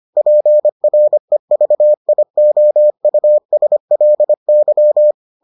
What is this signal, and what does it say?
Morse code, heard as a keyed tone, about 25 wpm, 600 Hz. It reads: PREVIOUSLY